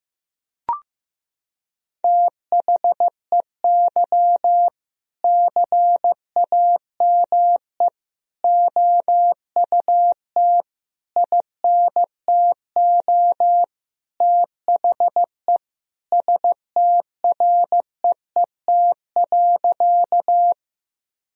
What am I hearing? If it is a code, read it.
Morse code, heard as a keyed tone, about 15 wpm, 700 Hz. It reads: THEY CAME OUT INTO THE STREET.
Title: Sentences from Top 1000 Words - Review Part 3 - 15wpm